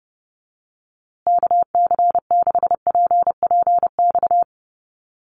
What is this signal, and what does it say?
Morse code, heard as a keyed tone, about 30 wpm, 700 Hz. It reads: KC6PPX
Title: USA Call Signs Repeated 3x 30wpm